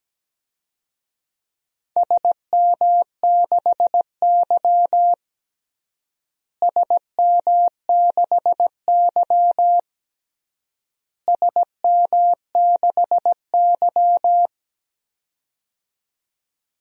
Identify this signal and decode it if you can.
Morse code, heard as a keyed tone, about 17 wpm, 700 Hz. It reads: SM6Y SM6Y SM6Y
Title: DXCC-100 Call Signs - 4 Characters - Repeated 3x - 17wpm